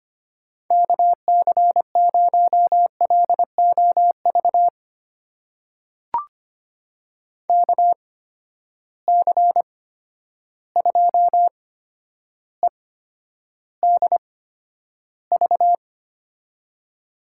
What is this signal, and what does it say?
Morse code, heard as a keyed tone, about 25 wpm, spaced as for 11 wpm, 700 Hz. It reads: KC0LOV KC2EDV
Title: USA Call Signs - 6 Characters - 8x Character Spacing - 25wpm